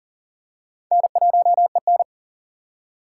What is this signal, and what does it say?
Morse code, heard as a keyed tone, about 39 wpm, 700 Hz. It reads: N1EN
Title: K1USN SST QSO Practice - 40wpm